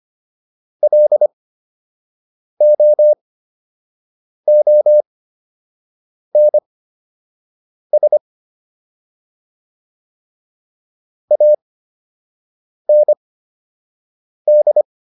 Morse code, heard as a keyed tone, about 25 wpm, spaced as for 6 wpm, 600 Hz. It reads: LOONS AND